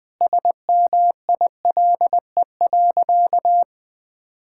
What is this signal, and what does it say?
Morse code, heard as a keyed tone, about 21 wpm, 700 Hz. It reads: SMILE.
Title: Sentences with 2-Word Phrases - Spoken Beginning - 20wpm